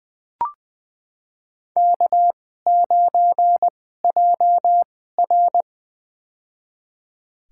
Morse code, heard as a keyed tone, about 20 wpm, spaced as for 14 wpm, 700 Hz. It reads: K9JR
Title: USA Call Signs - 4 Characters - 2x Character Spacing - 20wpm